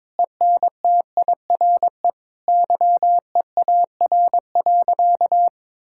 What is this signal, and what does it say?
Morse code, heard as a keyed tone, about 22 wpm, 700 Hz. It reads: ENTIRE YEAR.
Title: Sentences from Top 1000 Words - Review Part 5 - 22wpm